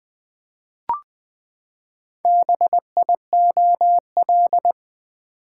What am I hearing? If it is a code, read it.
Morse code, heard as a keyed tone, about 20 wpm, 700 Hz. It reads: BIOL
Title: Build A Word - 4 Letters At A Time - 10-Letter Words - 20wpm